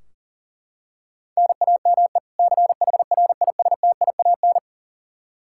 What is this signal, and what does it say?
Morse code, heard as a keyed tone, about 40 wpm, 700 Hz. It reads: NAME CHRISTIAN
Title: Practice QSO - Exchange 1 - Beginning - 40wpm